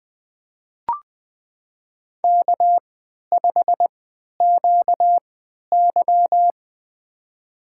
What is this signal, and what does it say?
Morse code, heard as a keyed tone, about 20 wpm, spaced as for 11 wpm, 700 Hz. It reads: K5QY
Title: USA Call Signs - 4 Characters - 3x Character Spacing - 20wpm